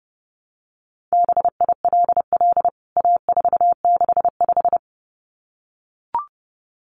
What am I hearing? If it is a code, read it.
Morse code, heard as a keyed tone, about 30 wpm, 700 Hz. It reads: BILL A465
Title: CWT Exchange 30wpm